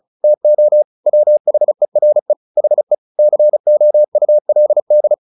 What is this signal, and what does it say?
Morse code, heard as a keyed tone, about 37 wpm, 600 Hz. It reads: TO WHERE HE COULD
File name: the old man and the sea 2 35wpm